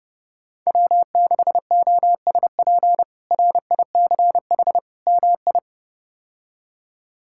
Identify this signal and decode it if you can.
Morse code, heard as a keyed tone, about 30 wpm, 700 Hz. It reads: W6OSP RICH MI
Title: K1USN SST QSO Practice - 30wpm